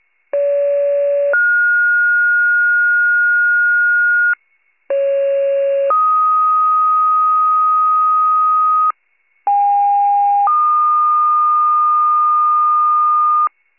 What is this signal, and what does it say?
Two-tone page, heard as a one-second tone, then a three-second tone, about 570 then 1430 Hz, about 550 then 1250 Hz, about 810 then 1220 Hz. It is heard from 0.3 to 13.5 s.